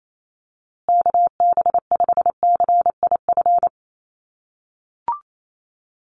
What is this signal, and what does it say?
Morse code, heard as a keyed tone, about 28 wpm, 700 Hz. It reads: KB5CIF
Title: USA Call Signs Repeated 3x 28wpm